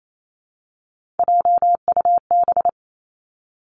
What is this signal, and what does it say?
Morse code, heard as a keyed tone, about 28 wpm, 700 Hz. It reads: JUB
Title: Three Challenging Letter-Number Combinations 28wpm